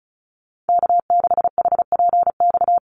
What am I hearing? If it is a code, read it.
Morse code, heard as a keyed tone, about 34 wpm, 700 Hz. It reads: K6HPX
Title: NA SOTA Activator Callsigns 35wpm